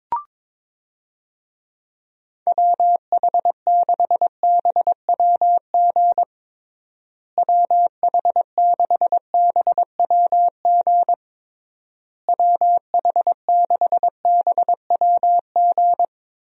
Morse code, heard as a keyed tone, about 22 wpm, 700 Hz. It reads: WH6BWG WH6BWG WH6BWG
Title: USA Call Signs - 6 Characters - Repeated 3x - 22wpm